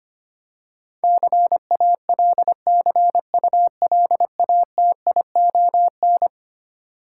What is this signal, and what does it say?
Morse code, heard as a keyed tone, about 25 wpm, 700 Hz. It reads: CALCULATION
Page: morse-code-ninja-practice-25wpm